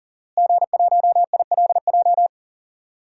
Morse code, heard as a keyed tone, about 40 wpm, 690 Hz. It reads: G1ILJ
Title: DXCC-100 Call Signs - 5 Characters - Repeated 2x - 40wpm